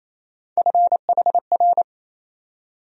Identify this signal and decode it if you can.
Morse code, heard as a keyed tone, about 28 wpm, 700 Hz. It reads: FHR
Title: USA Call Sign Postfixes - 3 Letters 28wpm